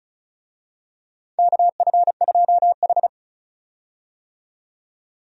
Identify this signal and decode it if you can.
Morse code, heard as a keyed tone, about 34 wpm, 700 Hz. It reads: KF2H
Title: USA Call Signs - 4 Characters - 35wpm